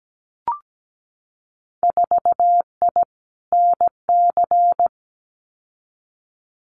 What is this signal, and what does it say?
Morse code, heard as a keyed tone, about 17 wpm, 700 Hz. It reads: 4I NC
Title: Winter Field Day Exchanges 17wpm